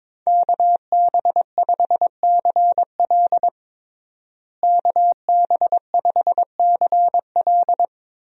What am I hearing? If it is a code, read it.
Morse code, heard as a keyed tone, about 22 wpm, 700 Hz. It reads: KB5CL KB5CL
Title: USA Call Signs - 5 Characters - Repeated 3x - 22wpm